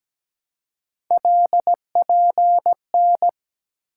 Morse code, heard as a keyed tone, about 17 wpm, 700 Hz. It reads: LPN
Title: Practice QSO - Exchange 3 - More Info - 17wpm